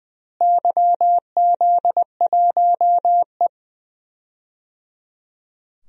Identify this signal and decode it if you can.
Morse code, heard as a keyed tone, about 21 wpm, 700 Hz. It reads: YZ1E
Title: DXCC-100 Call Signs - 4 Characters - Repeated 3x - 20wpm